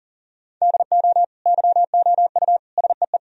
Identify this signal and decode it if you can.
Morse code, heard as a keyed tone, about 40 wpm, 700 Hz. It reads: DO YOU SEE
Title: Sentences from Top 1000 Words - Review Part 2 - 40wpm